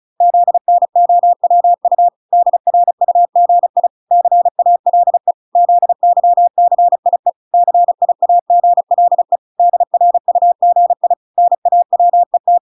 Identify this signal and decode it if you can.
Morse code, heard as a keyed tone, about 34 wpm, 700 Hz. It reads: ZNOWU DRUGI CALE ZYCIE CIAGLE DRUGI NAWET